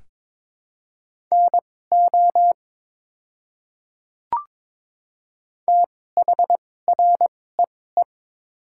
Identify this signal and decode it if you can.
Morse code, heard as a keyed tone, about 23 wpm, spaced as for 16 wpm, 700 Hz. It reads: NO THREE
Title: 2x Character Spacing - Top 200 Words - 22wpm